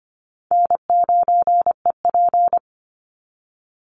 Morse code, heard as a keyed tone, about 25 wpm, 700 Hz. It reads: N9EP
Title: Easy Call Signs Repeated 2x - 25wpm